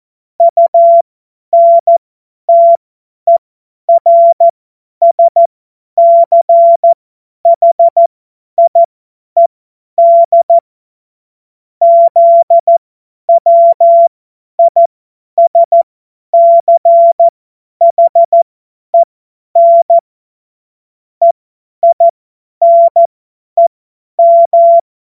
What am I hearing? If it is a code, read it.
Morse code, heard as a keyed tone, about 14 wpm, spaced as for 10 wpm, 680 Hz. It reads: UNTERSCHIED ZWISCHEN EINEM
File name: Morsekurs / Witze / Witz 16. August 2022 50 F70